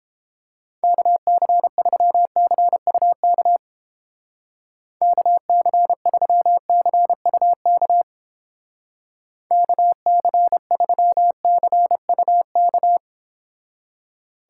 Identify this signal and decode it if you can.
Morse code, heard as a keyed tone, about 28 wpm, 700 Hz. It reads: KC3CUK KC3CUK KC3CUK